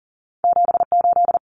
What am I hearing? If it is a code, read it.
Morse code, heard as a keyed tone, about 40 wpm, 700 Hz. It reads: 78
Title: Common Cut Numbers - 2-Digit 40wpm